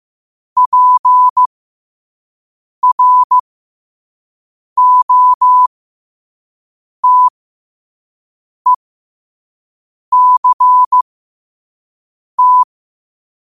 Morse code, heard as a keyed tone, about 15 wpm, spaced as for 5 wpm, 1000 Hz. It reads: PROTECT
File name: qotdfiles / 5wpm